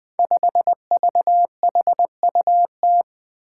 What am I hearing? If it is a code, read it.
Morse code, heard as a keyed tone, about 20 wpm, 700 Hz. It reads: 5VHUT
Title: Five Challenging Letter-Number Combinations 20wpm